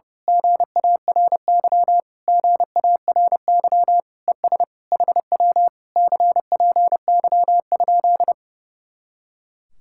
Morse code, heard as a keyed tone, about 29 wpm, 700 Hz. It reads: GARY GARY ES HW CPY?
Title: Speed-Racing Common QSO Phrases - 30wpm